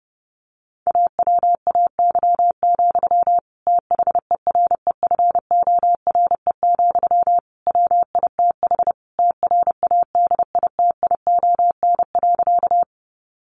Morse code, heard as a keyed tone, about 30 wpm, 700 Hz. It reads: AWAY, THEREFORE, WITH TRADITION.
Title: Sentences from Top 1500 Words 30wpm